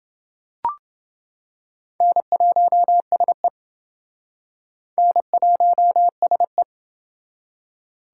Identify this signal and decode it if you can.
Morse code, heard as a keyed tone, about 27 wpm, 700 Hz. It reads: N1SE N1SE